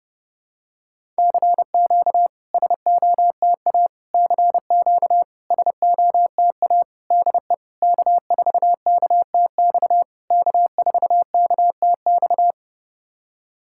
Morse code, heard as a keyed tone, about 32 wpm, 700 Hz. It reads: CQ SOTA CQ SOTA DE K4KTX K4KTX
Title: SOTA QSO Practice 30wpm